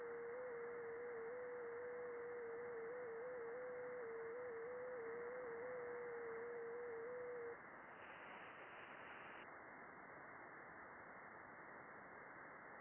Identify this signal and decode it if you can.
FT8.
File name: contests / june-vhf-k5tr- -0200